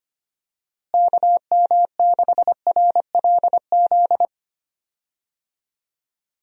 Morse code, heard as a keyed tone, about 25 wpm, 700 Hz. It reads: KM6RLZ
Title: USA Call Signs - 6 Characters - Repeated 3x - 25wpm